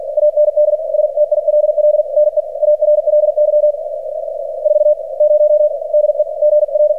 Morse code, heard as a keyed tone, about 47 wpm, spaced as for 40 wpm, 600 Hz, with radio noise. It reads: AND STILL NEARLY 40 DEGR
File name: archived / CRSnet